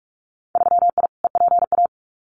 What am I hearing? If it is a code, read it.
Morse code, heard as a keyed tone, about 44 wpm, 700 Hz. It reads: 3I EPA
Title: Winter Field Day Exchanges 45wpm